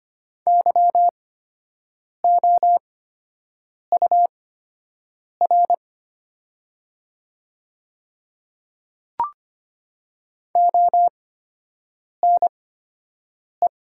Morse code, heard as a keyed tone, about 25 wpm, spaced as for 7 wpm, 690 Hz. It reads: YOUR ONE
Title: 8x Character Spacing - Top 100 Words - 25wpm